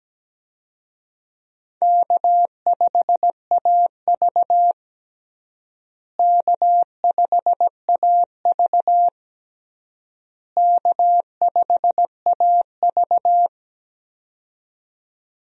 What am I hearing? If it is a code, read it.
Morse code, heard as a keyed tone, about 17 wpm, 700 Hz. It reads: K5AV K5AV K5AV
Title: USA Call Signs - 4 Characters - Repeated 3x - 17wpm